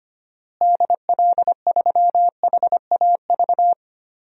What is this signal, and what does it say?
Morse code, heard as a keyed tone, about 25 wpm, 700 Hz. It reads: DL3HAV
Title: DXCC-100 Call Signs - 6 Characters - Repeated 2x - 25wpm